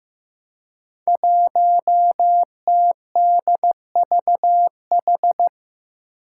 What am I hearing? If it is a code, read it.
Morse code, heard as a keyed tone, about 15 wpm, 700 Hz. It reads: 1TDVH
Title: Five Challenging Letter-Number Combinations 15wpm